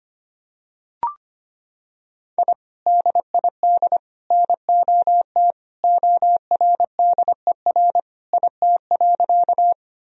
Morse code, heard as a keyed tone, about 25 wpm, 700 Hz. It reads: I DID NOT ORDER IT.
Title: Sentences from Top 1000 Words - Review Part 2 - 25wpm